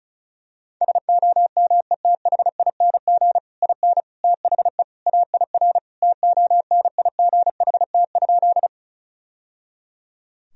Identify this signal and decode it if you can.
Morse code, heard as a keyed tone, about 34 wpm, 700 Hz. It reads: SOMETHING IN THE AIR TONIGHT?
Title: Sentences from Top 1000 Words - Review Part 1 - 35wpm